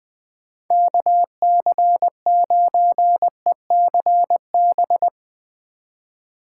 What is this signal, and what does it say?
Morse code, heard as a keyed tone, about 20 wpm, 700 Hz. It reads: KC9ECB
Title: Speed Racing - USA 2x3 Call Signs 20wpm